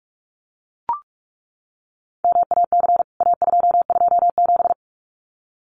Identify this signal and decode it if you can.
Morse code, heard as a keyed tone, about 44 wpm, 700 Hz. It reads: MAC A227
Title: CWT Exchange 45wpm